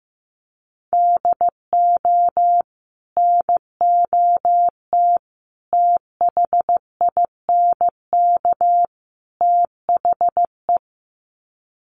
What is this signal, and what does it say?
Morse code, heard as a keyed tone, about 14 wpm, 700 Hz. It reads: DO NOT THINK THE
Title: Sets of 4 Words 15wpm